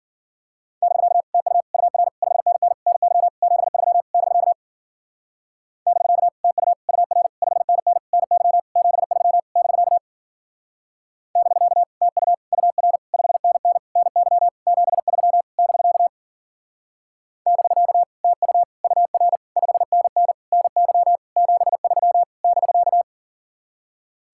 Morse code, heard as a keyed tone, about 49 wpm, 700 Hz. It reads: <BK> TU UR 5NN NY 73 <BK> <BK> TU UR 5NN NY 73 <BK> <BK> TU UR 5NN NY 73 <BK> <BK> TU UR 5NN NY 73 <BK>
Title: Speed-Racing Common QSO Phrases - 40wpm